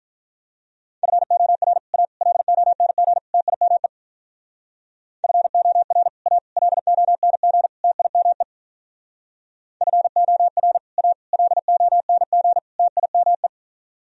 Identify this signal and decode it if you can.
Morse code, heard as a keyed tone, about 44 wpm, 700 Hz. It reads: FOR A LONG TIME FOR A LONG TIME FOR A LONG TIME
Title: Speed-Racing - Sets of 4 Words 35wpm